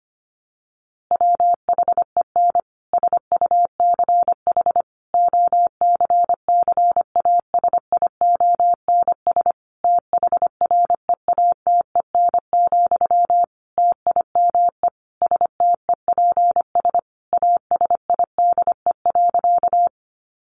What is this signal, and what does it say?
Morse code, heard as a keyed tone, about 25 wpm, 700 Hz. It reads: WHEN SUCH OCCASIONS THREATEN, TIME STEPS ASIDE.
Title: A Sound Of Thunder 25wpm